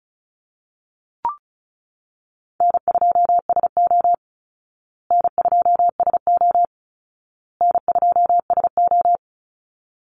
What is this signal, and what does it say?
Morse code, heard as a keyed tone, about 34 wpm, 700 Hz. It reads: N2SO N2SO N2SO
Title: Easy Call Signs Repeated 3x - 35wpm